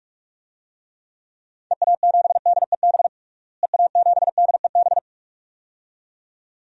Morse code, heard as a keyed tone, about 43 wpm, 700 Hz. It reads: EA7DEB EA7DEB
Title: DXCC-100 Call Signs - 6 Characters - Repeated 2x - 45wpm